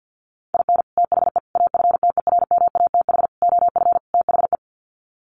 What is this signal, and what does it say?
Morse code, heard as a keyed tone, about 48 wpm, 700 Hz. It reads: IN THE AFTERMATH OF THE
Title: Sets of 5 Words 50wpm